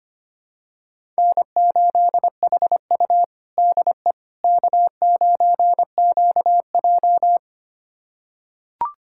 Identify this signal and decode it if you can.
Morse code, heard as a keyed tone, about 25 wpm, 700 Hz. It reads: N8HU DE K9QJ
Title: USA Callsign DE Callsign - 4 Characters - 25wpm